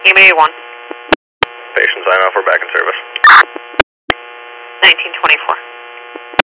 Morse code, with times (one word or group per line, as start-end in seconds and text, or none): none